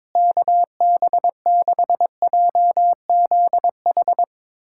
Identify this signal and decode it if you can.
Morse code, heard as a keyed tone, about 22 wpm, 700 Hz. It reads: KB6JZH
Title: USA Callsign DE Callsign - 6 Characters - 22wpm